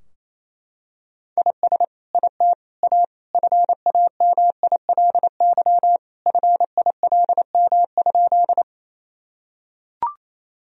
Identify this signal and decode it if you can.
Morse code, heard as a keyed tone, about 27 wpm, 700 Hz. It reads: IS IT A FAMILY FILM?
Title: Sentences from Top 1000 Words - Review Part 3 - 28wpm